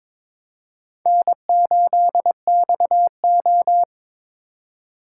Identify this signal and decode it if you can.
Morse code, heard as a keyed tone, about 26 wpm, 700 Hz. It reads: N8XO
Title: Speed Racing - USA 1x2 Call Signs 22wpm